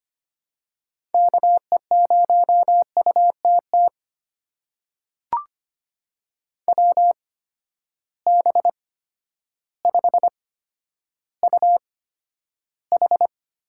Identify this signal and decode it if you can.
Morse code, heard as a keyed tone, about 25 wpm, spaced as for 11 wpm, 700 Hz. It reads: KE0UTT WB5UH
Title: USA Call Signs - 6 Characters - 8x Character Spacing - 25wpm